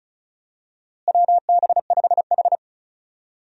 Morse code, heard as a keyed tone, about 36 wpm, 700 Hz. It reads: WB5H
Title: USA Call Signs - 4 Characters - Repeated 3x - 35wpm